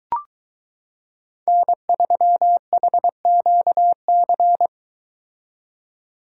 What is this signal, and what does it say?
Morse code, heard as a keyed tone, about 23 wpm, 700 Hz. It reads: N3HQC